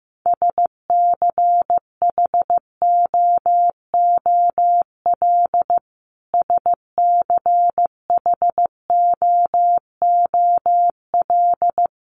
Morse code, heard as a keyed tone, about 15 wpm, 700 Hz. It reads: SCHOOL SCHOOL
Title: Top 500 Words Repeated 15wpm